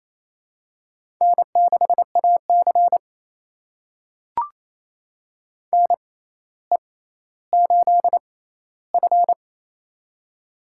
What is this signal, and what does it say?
Morse code, heard as a keyed tone, about 28 wpm, spaced as for 14 wpm, 700 Hz. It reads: N6AC NE8F
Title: USA Call Signs - 4 Characters - 6x Character Spacing - 28wpm